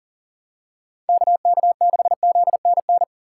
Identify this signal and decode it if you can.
Morse code, heard as a keyed tone, about 37 wpm, 700 Hz. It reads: KK6ZNN
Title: USA Call Signs - 6 Characters - 6x Character Spacing - 40wpm